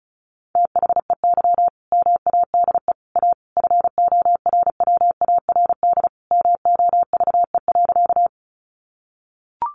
Morse code, heard as a keyed tone, about 34 wpm, 700 Hz. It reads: THEY MADE A FORWARD MOVE.
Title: Sentences from Top 1300 Words 35wpm